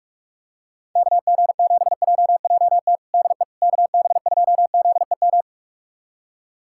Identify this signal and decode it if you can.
Morse code, heard as a keyed tone, about 46 wpm, 700 Hz. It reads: KG7JJT DE KB2ZEM
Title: USA Callsign DE Callsign - 6 Characters - 45wpm